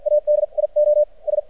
Morse code, heard as a keyed tone, about 56 wpm, spaced as for 38 wpm, 600 Hz. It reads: ADIOS
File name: archived / CRSnet